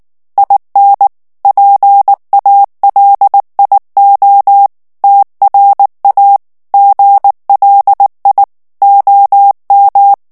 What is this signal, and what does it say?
Morse code, heard as a keyed tone, about 19 wpm, 800 Hz. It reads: IN PALIO TRA GLI OM
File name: morse mp3 / ITALIEN texte clair 2 vitesse 18 mots minute